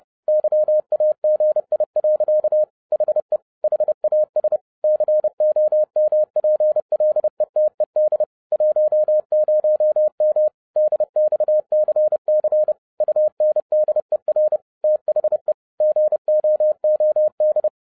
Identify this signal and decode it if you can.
Morse code, heard as a keyed tone, about 30 wpm, 610 Hz. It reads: YAGI. HE HAS COMPLETED 10M DXCC UNDER THE GOOD